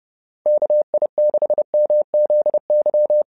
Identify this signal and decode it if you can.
Morse code, heard as a keyed tone, about 30 wpm, 600 Hz. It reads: KI6MZY
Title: SOTA QSO Practice 30wpm